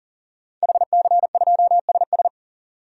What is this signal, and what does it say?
Morse code, heard as a keyed tone, about 37 wpm, 700 Hz. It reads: HC2SS
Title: DXCC-100 Call Signs - 5 Characters - 40wpm